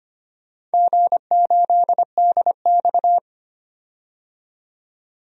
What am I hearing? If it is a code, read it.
Morse code, heard as a keyed tone, about 25 wpm, 700 Hz. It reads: G8DX
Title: DXCC-100 Call Signs - 4 Characters - 25wpm